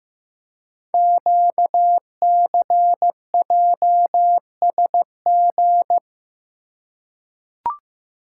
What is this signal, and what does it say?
Morse code, heard as a keyed tone, about 15 wpm, 700 Hz. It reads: QCJSG
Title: Code Groups - 5 Letters - 15wpm